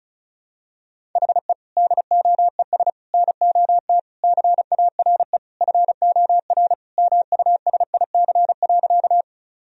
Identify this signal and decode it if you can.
Morse code, heard as a keyed tone, about 36 wpm, 700 Hz. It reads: HE DOES NOT CARE FOR MUSIC.
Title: Sentences from Top 1000 Words - Review Part 1 - 35wpm